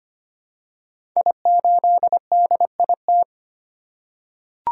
Morse code, heard as a keyed tone, about 25 wpm, 700 Hz. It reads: I8DIT
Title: Code Groups - 5 Letter:Numbers - 25wpm